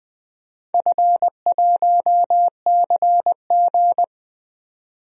Morse code, heard as a keyed tone, about 20 wpm, 700 Hz. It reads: F1CG
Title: DXCC-100 Call Signs - 4 Characters - 20wpm